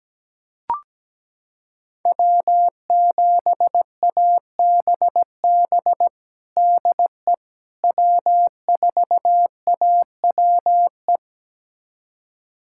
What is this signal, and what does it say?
Morse code, heard as a keyed tone, about 17 wpm, 700 Hz. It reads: W7ABB DE W4AWE
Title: USA Callsign DE Callsign - 5 Characters - 17wpm